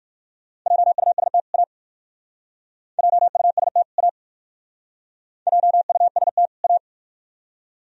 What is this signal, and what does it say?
Morse code, heard as a keyed tone, about 51 wpm, 690 Hz. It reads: JUST A JUST A JUST A
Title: Speed-Racing - Sets of 2 Words 45wpm